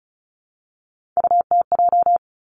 Morse code, heard as a keyed tone, about 36 wpm, 700 Hz. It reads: UTJ
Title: Three Challenging Letter-Number Combinations 35wpm